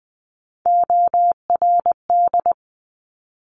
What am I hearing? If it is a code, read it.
Morse code, heard as a keyed tone, about 20 wpm, 700 Hz. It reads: ORD
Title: Top Three Letter Combinations in Words 20wpm